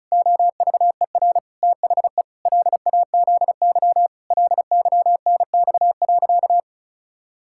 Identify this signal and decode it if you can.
Morse code, heard as a keyed tone, about 34 wpm, 700 Hz. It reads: OVER THE LAZY LYNX.
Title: Pangram Challenge v2 - 35wpm